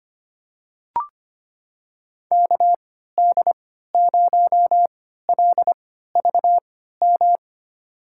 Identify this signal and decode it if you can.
Morse code, heard as a keyed tone, about 25 wpm, spaced as for 14 wpm, 700 Hz. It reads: KD0LVM
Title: USA Call Signs - 6 Characters - 3x Character Spacing - 25wpm